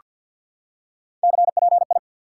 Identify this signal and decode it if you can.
Morse code, heard as a keyed tone, about 48 wpm, 700 Hz. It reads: CPI
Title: Top 90 QSO Elements 50wpm